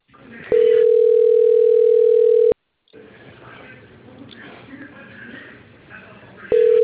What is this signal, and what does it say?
Ringback tone (North America), 440 + 480 Hz.